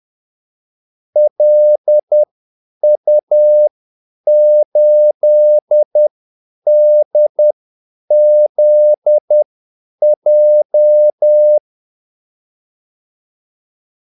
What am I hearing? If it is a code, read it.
Morse code, heard as a keyed tone, about 10 wpm, spaced as for 8 wpm, 600 Hz. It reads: LU8DZJ
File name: Callsigns-10-8wpm